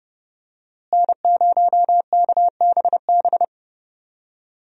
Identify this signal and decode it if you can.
Morse code, heard as a keyed tone, about 30 wpm, 700 Hz. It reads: N0KBB
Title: USA Call Signs - 5 Characters - Repeated 2x - 30wpm